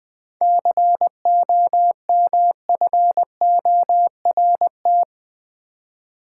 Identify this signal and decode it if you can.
Morse code, heard as a keyed tone, about 20 wpm, 700 Hz. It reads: COMFORT
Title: Seven Letter Words - 20wpm